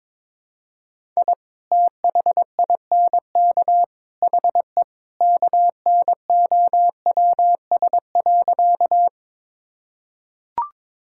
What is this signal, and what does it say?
Morse code, heard as a keyed tone, about 22 wpm, 700 Hz. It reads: I THINK HE KNOWS.
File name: practice / 22wpm / Sentences with 3-Word Phrases - 22wpm